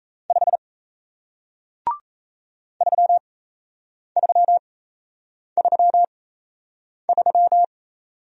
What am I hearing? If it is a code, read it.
Morse code, heard as a keyed tone, about 38 wpm, spaced as for 8 wpm, 700 Hz. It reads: 5 3333